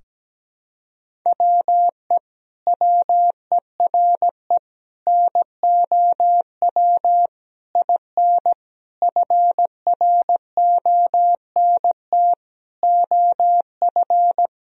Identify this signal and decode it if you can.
Morse code, heard as a keyed tone, about 17 wpm, 700 Hz. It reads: WE WERE NOW IN FRONT OF
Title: Sentences from Top 1000 Words - Review Part 2 - 17wpm